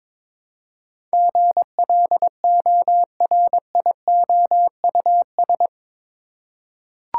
Morse code, heard as a keyed tone, about 22 wpm, 700 Hz. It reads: GLORIOUS
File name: practice / 22wpm / Eight Letter Words - 22wpm